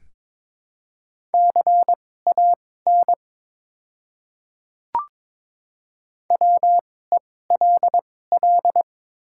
Morse code, heard as a keyed tone, about 22 wpm, spaced as for 16 wpm, 700 Hz. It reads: CAN WELL
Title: 2x Character Spacing - Top 200 Words - 22wpm